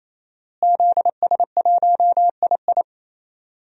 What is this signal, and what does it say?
Morse code, heard as a keyed tone, about 27 wpm, 700 Hz. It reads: ZS1II
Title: DXCC-100 Call Signs - 5 Characters - Repeated 3x - 28wpm